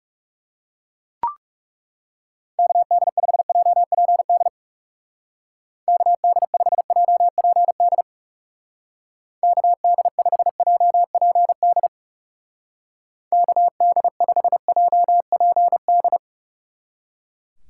Morse code, heard as a keyed tone, about 37 wpm, 700 Hz. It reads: KD5JPD KD5JPD KD5JPD KD5JPD
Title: Speed Racing - USA 2x3 Call Signs 30wpm